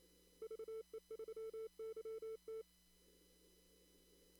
Morse code, heard as a keyed tone, about 26 wpm, 440 Hz. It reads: VE3YT
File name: CW / kt5j wpx cw-op-k5pi